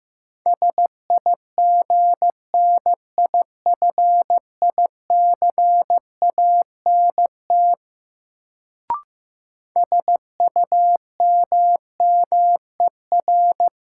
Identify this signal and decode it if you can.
Morse code, heard as a keyed tone, about 15 wpm, 700 Hz. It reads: SIGNIFICANT SUMMER
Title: Sentences from Top 1000 Words - Review Part 6 - New Word Drill - 15wpm